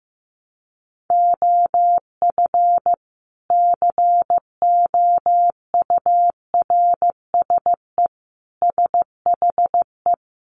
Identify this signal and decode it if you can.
Morse code, heard as a keyed tone, about 15 wpm, 700 Hz. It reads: OF COURSE SHE
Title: Sentences from Top 500 Words 15wpm